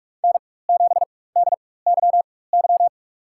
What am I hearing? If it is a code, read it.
Morse code, heard as a keyed tone, about 42 wpm, spaced as for 20 wpm, 700 Hz. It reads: N7DYY
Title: USA Call Signs - 5 Characters - 4x Character Spacing - 45wpm